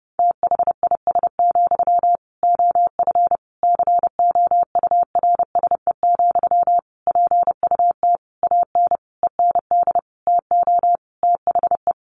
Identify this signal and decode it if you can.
Morse code, heard as a keyed tone, about 30 wpm, 700 Hz. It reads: THIS, OF COURSE, PUT AN END TO THE
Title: Sentences from Top 1400 Words 30wpm